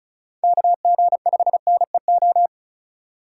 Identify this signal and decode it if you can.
Morse code, heard as a keyed tone, about 43 wpm, 700 Hz. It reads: KG5NEO
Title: Speed Racing - USA 2x3 Call Signs 35wpm